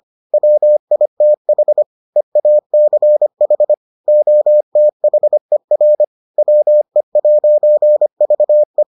Morse code, heard as a keyed tone, about 25 wpm, 600 Hz. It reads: WITH EACH OTHER WE'VE